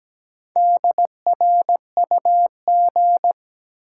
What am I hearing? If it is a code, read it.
Morse code, heard as a keyed tone, about 17 wpm, 700 Hz. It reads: DRUG
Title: Sets of 5 Words - New Word Drill - 17wpm